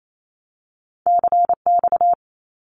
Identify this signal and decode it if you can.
Morse code, heard as a keyed tone, about 28 wpm, 700 Hz. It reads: CX
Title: Abbreviations 28wpm